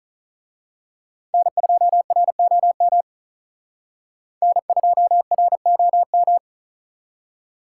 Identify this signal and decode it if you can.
Morse code, heard as a keyed tone, about 39 wpm, 700 Hz. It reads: N2ROM N2ROM
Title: Speed Racing - USA 1x3 Call Signs 35wpm